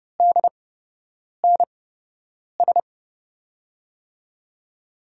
Morse code, heard as a keyed tone, about 31 wpm, spaced as for 8 wpm, 700 Hz. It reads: DNS
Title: USA Call Signs - 5 Characters - 8x Character Spacing - 30wpm